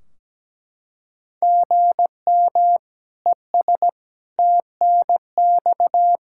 Morse code, heard as a keyed tone, about 17 wpm, 700 Hz. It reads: GM ES TNX
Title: Practice QSO - Exchange 1 - Beginning - 17wpm